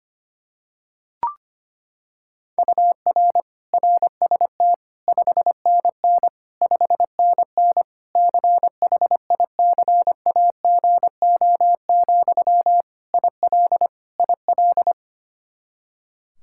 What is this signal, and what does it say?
Morse code, heard as a keyed tone, about 25 wpm, 700 Hz. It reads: UR RST 5NN 5NN CHICAGO, IL IL
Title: K1USN SST QSO Practice 25wpm